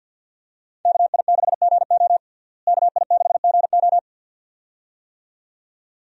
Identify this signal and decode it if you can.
Morse code, heard as a keyed tone, about 50 wpm, 700 Hz. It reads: KI6GO KI6GO
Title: USA Call Signs - 5 Characters - Repeated 2x - 50wpm